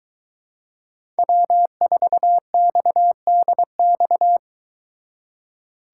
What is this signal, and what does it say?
Morse code, heard as a keyed tone, about 23 wpm, 700 Hz. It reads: W4XDX